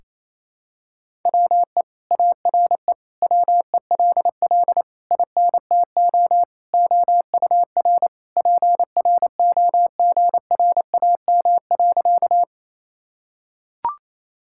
Morse code, heard as a keyed tone, about 27 wpm, 700 Hz. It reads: WE ARE WELL INTO OUR PROGRAM.
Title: Sentences from Top 1000 Words - Review Part 5 - 28wpm